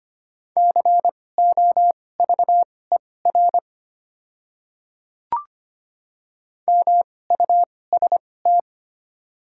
Morse code, heard as a keyed tone, about 25 wpm, spaced as for 18 wpm, 700 Hz. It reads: COVER MUST
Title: 2x Character Spacing - Top 300 Words - 25wpm